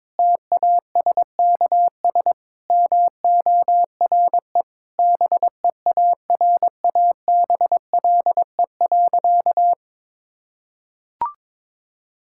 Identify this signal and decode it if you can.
Morse code, heard as a keyed tone, about 22 wpm, 700 Hz. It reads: TASKS MORE BEARABLE.
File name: Difficult Sentences with Hints - 22wpm